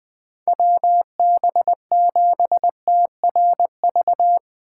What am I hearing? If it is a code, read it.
Morse code, heard as a keyed tone, about 20 wpm, 700 Hz. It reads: WB7TRV
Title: USA Call Signs - 6 Characters - 3x Character Spacing - 20wpm